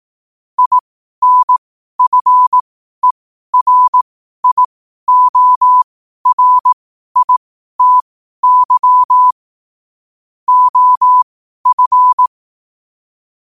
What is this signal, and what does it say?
Morse code, heard as a keyed tone, about 18 wpm, spaced as for 12 wpm, 1000 Hz. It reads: INFERIORITY OF
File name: qotdfiles / 12wpm